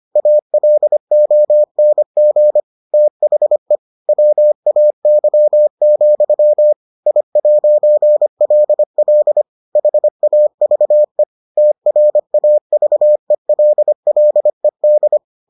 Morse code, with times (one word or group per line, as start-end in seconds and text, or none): 0.2-2.6 ALONG
2.9-3.8 THE
4.1-6.7 WAY,
7.1-9.4 I'LL
9.8-11.2 HAVE
11.6-15.2 TRAVELLED